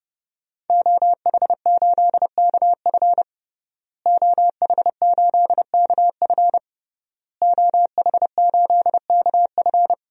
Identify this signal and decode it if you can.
Morse code, heard as a keyed tone, about 30 wpm, 700 Hz. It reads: OH8KF OH8KF OH8KF
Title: DXCC-100 Call Signs - 5 Characters - Repeated 3x - 30wpm